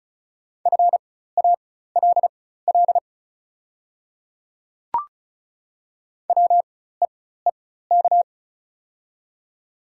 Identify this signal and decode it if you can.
Morse code, heard as a keyed tone, about 34 wpm, spaced as for 16 wpm, 700 Hz. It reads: FALL WEEK
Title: 4x Character Spacing - Top 500 Words - 35wpm